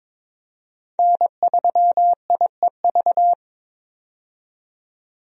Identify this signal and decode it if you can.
Morse code, heard as a keyed tone, about 22 wpm, 700 Hz. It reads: N3IEV
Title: USA Call Signs - 5 Characters - 22wpm